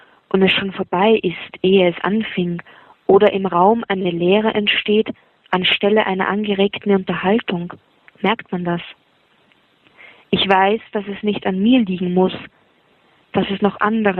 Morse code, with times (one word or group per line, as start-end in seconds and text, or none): none